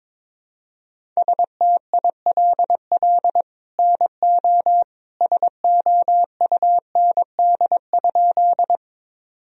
Morse code, heard as a keyed tone, about 22 wpm, 700 Hz. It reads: STILL NO SOUND?
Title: Sentences from Top 1000 Words - Review Part 1 - 22wpm